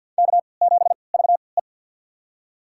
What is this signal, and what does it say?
Morse code, heard as a keyed tone, about 47 wpm, spaced as for 28 wpm, 700 Hz. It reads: K7VE
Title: USA Call Signs - 3x Character Spacing - 50wpm